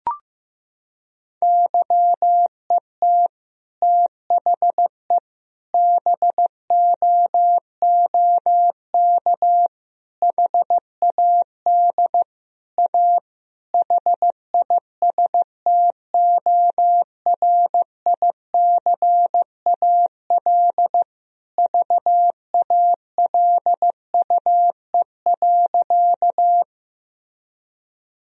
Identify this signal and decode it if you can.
Morse code, heard as a keyed tone, about 16 wpm, 700 Hz. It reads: YET THE BOOK HAD A HISTORICAL VALUE.
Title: Sentences from Top 1000 Words - Review Part 4 - 15wpm